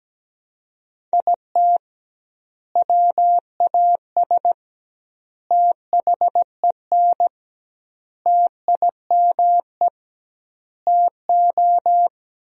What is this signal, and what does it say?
Morse code, heard as a keyed tone, about 17 wpm, 700 Hz. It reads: IT WAS THEN TIME TO
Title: 2x Word-Spacing - Sentences from Top 300 Words - 17wpm